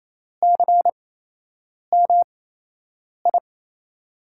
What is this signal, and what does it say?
Morse code, heard as a keyed tone, about 27 wpm, spaced as for 8 wpm, 700 Hz. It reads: CMI
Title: USA Call Signs - 6 Characters - 8x Character Spacing - 28wpm